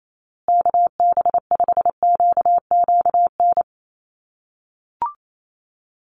Morse code, heard as a keyed tone, about 28 wpm, 700 Hz. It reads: KB5QQN
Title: USA Call Signs Repeated 3x 28wpm